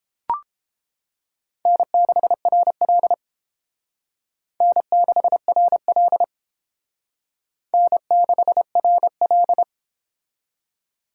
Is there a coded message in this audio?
Yes: Morse code from 1.7 to 9.6 s, heard as a keyed tone, about 29 wpm, 700 Hz. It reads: N6RL N6RL N6RL